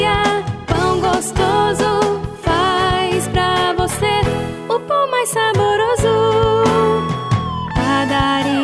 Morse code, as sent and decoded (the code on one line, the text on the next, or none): none